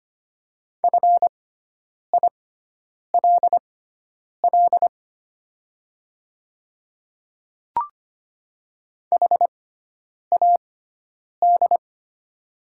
Morse code, heard as a keyed tone, about 25 wpm, spaced as for 9 wpm, 690 Hz. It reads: FILL HAD
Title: 6x Character Spacing - Top 500 Words - 25wpm